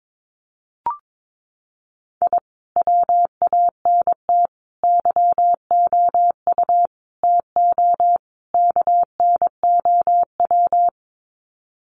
Morse code, heard as a keyed tone, about 22 wpm, 700 Hz. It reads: I WANT YOU TO KNOW
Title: Sets of 5 Words Encore 22wpm